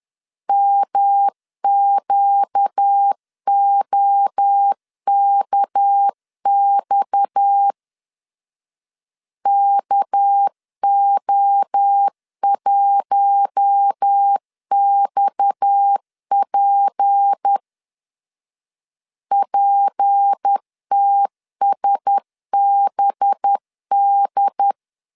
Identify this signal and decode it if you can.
Morse code, heard as a keyed tone, about 10 wpm, 800 Hz. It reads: MQOKX KO1XP PTSBD
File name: LEZIONI / 6 PKX0 / 6 PKX 10wpm 2426